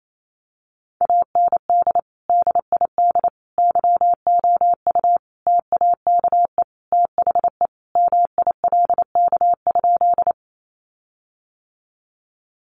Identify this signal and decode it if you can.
Morse code, heard as a keyed tone, about 28 wpm, 700 Hz. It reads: AND DID YOU TAKE THE MILK?
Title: Sentences from Top 1500 Words 28wpm